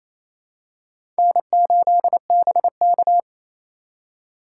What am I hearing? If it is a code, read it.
Morse code, heard as a keyed tone, about 28 wpm, 700 Hz. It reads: N8BK
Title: USA Call Signs - 4 Characters - 3x Character Spacing - 28wpm